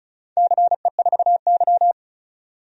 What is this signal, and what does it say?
Morse code, heard as a keyed tone, about 34 wpm, 700 Hz. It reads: CE4Y
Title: DXCC-100 Call Signs - 4 Characters - 35wpm